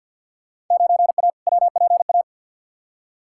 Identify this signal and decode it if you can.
Morse code, heard as a keyed tone, about 51 wpm, 700 Hz. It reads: 9A WPA
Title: Field Day Exchanges Encore - 50wpm